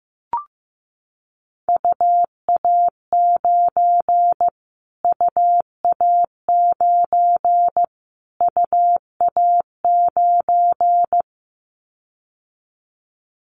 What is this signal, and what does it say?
Morse code, heard as a keyed tone, about 15 wpm, 700 Hz. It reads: UA9 UA9 UA9
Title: Callsign Prefixes for Top 25 DXCC - Repeated 3x - 15wpm